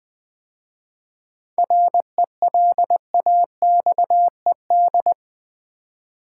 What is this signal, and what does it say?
Morse code, heard as a keyed tone, about 21 wpm, 700 Hz. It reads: RELAXED
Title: 7 Letter Words - First 3 Letters Given - 20wpm